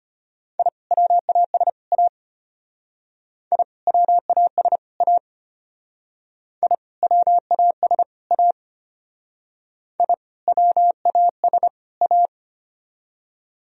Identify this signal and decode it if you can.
Morse code, heard as a keyed tone, about 32 wpm, 700 Hz. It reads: I WAS A I WAS A I WAS A I WAS A